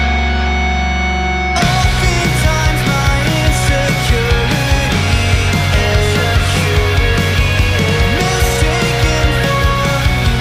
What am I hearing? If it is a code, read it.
Dial-up modem handshake.